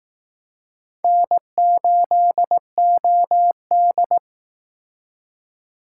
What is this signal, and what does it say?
Morse code, heard as a keyed tone, about 19 wpm, 700 Hz. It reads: N8OD